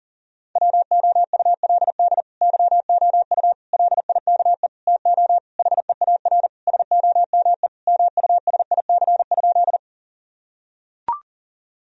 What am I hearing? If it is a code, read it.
Morse code, heard as a keyed tone, about 40 wpm, 700 Hz. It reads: WOULD YOU LIKE TO HEAR SOME MUSIC?
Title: Sentences from Top 1000 Words - Review Part 1 - 40wpm